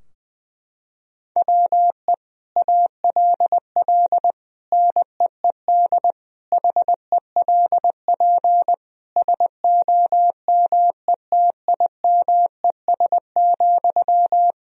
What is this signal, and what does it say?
Morse code, heard as a keyed tone, about 21 wpm, 700 Hz. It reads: WE ALL NEED HELP SOMETIMES,
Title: Sentences from Top 1000 Words - Review Part 3 - 20wpm